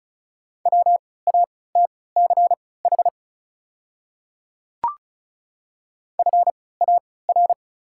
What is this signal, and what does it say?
Morse code, heard as a keyed tone, about 34 wpm, spaced as for 20 wpm, 700 Hz. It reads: WATCH FAR
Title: 3x Character Spacing - Top 300 Words - 35wpm